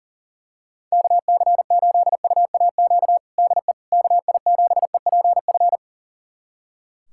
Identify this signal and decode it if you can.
Morse code, heard as a keyed tone, about 37 wpm, 700 Hz. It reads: KC8UAQ DE KI7EPF
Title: USA Callsign DE Callsign - 6 Characters - 40wpm